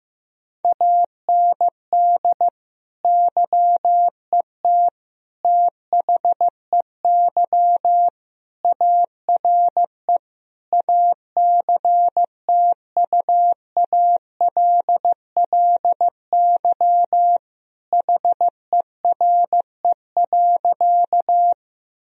Morse code, heard as a keyed tone, about 15 wpm, 700 Hz. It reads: AND YET THEY ARE ACTUALLY HERE.
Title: Sentences from Top 1000 Words - Review Part 4 - 15wpm